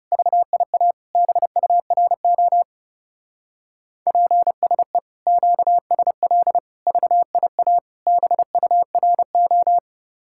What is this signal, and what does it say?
Morse code, heard as a keyed tone, about 37 wpm, 700 Hz. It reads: VIA BURO PSE QSL VIA BURO
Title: Speed-Racing Common QSO Phrases - 30wpm